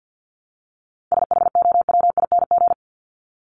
Morse code, heard as a keyed tone, about 50 wpm, 700 Hz. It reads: SHOWING
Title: ING Suffix - Top 100 Words with Base Words 50wpm